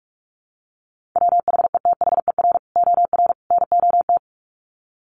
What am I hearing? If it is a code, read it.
Morse code, heard as a keyed tone, about 44 wpm, 700 Hz. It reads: WHETHER OR NOT
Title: Sets of 3 Words 45wpm